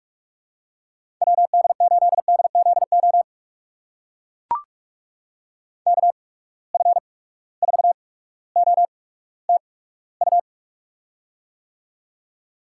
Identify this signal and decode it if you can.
Morse code, heard as a keyed tone, about 44 wpm, spaced as for 19 wpm, 700 Hz. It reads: WD8DZO KF4OTU
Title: USA Call Signs - 6 Characters - 8x Character Spacing - 45wpm